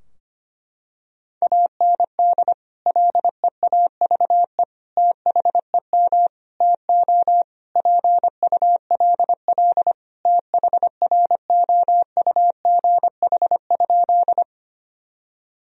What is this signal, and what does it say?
Morse code, heard as a keyed tone, about 25 wpm, 700 Hz. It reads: AND LEAVE THEM TO PULL THROUGH?
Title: Sentences from Top 1000 Words - Review Part 2 - 25wpm